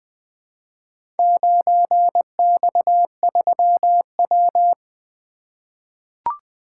Morse code, heard as a keyed tone, about 20 wpm, 700 Hz. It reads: 9X3W
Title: Four Letter-Number Combinations - 20wpm